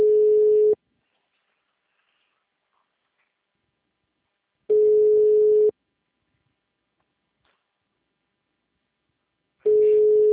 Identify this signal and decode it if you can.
Ringback tone (Europe, 425 Hz).
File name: out- -2022